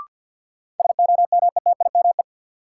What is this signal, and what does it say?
Morse code, heard as a keyed tone, about 47 wpm, 700 Hz. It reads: SOMETIME
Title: Sentences from Top 1000 Words - Review Part 6 - New Word Drill - 50wpm